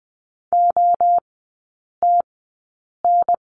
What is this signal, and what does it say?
Morse code, heard as a keyed tone, about 20 wpm, spaced as for 8 wpm, 700 Hz. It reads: OTN
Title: 035 Lesson 7e - Letter s - Three Characters 20wpm